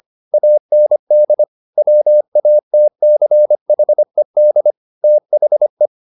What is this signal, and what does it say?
Morse code, heard as a keyed tone, about 25 wpm, 600 Hz. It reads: AND WATCHED THE